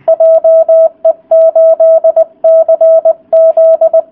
Morse code, heard as a keyed tone, about 20 wpm, 650 Hz. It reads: JE8CZ